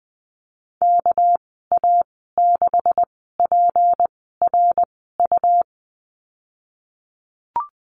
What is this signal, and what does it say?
Morse code, heard as a keyed tone, about 20 wpm, spaced as for 15 wpm, 700 Hz. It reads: KA6PRU
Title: 213 Lesson 34g - Number 8 - Call Signs 20wpm